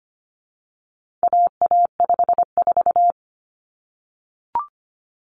Morse code, heard as a keyed tone, about 25 wpm, 700 Hz. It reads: AA54
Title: Common Cut Numbers - 4-Digit 25wpm